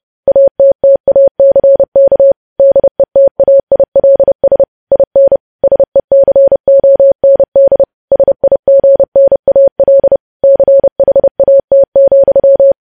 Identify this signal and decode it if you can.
Morse code, heard as a keyed tone, about 30 wpm, 550 Hz. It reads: ATTACK DETAILS IN SECOND SIGNAL CHAT,
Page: News Headlines in Morse Code at 30 WPM